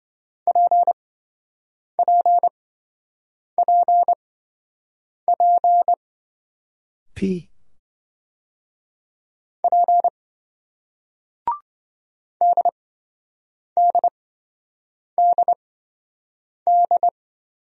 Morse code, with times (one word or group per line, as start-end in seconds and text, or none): 0.5-5.9 PPPP
9.6-10.1 P
12.4-17.1 DDDD